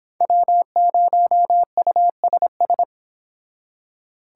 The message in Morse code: W0USS

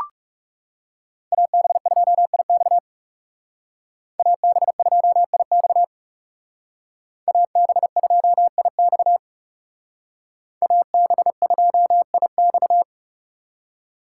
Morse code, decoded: AB2IX AB2IX AB2IX AB2IX

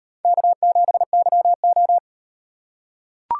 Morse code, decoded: K7YO

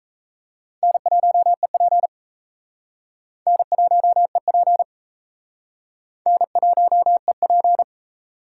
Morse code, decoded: N1EP N1EP N1EP

